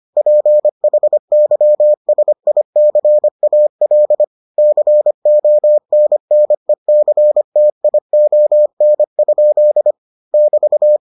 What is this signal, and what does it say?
Morse code, heard as a keyed tone, about 25 wpm, 600 Hz. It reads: PHYSICAL CONNECTION? <BT>